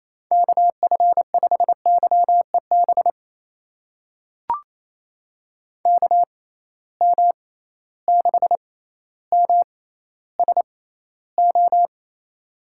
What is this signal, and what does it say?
Morse code, heard as a keyed tone, about 28 wpm, spaced as for 14 wpm, 700 Hz. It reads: KF5YEB KM6MSO
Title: USA Call Signs - 6 Characters - 6x Character Spacing - 28wpm